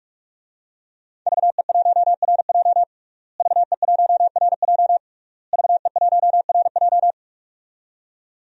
Morse code, heard as a keyed tone, about 41 wpm, 700 Hz. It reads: VE1RJ VE1RJ VE1RJ
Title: DXCC-100 Call Signs - 5 Characters - Repeated 3x - 45wpm